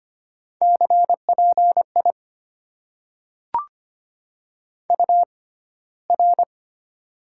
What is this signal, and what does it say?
Morse code, heard as a keyed tone, about 25 wpm, spaced as for 15 wpm, 700 Hz. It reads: CPI UR
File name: Top 90 QSO Elements - 6x Character Spacing - 25wpm